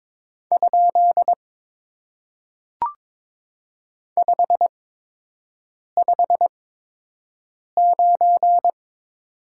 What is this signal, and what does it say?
Morse code, heard as a keyed tone, about 22 wpm, spaced as for 6 wpm, 700 Hz. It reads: ? 559
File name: Top 90 QSO Elements - 8x Character Spacing - 22wpm